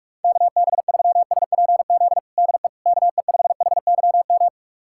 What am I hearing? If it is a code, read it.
Morse code, heard as a keyed tone, about 42 wpm, 690 Hz. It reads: KB3SPZ DE KE5HYM
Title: USA Callsign DE Callsign - 6 Characters - 45wpm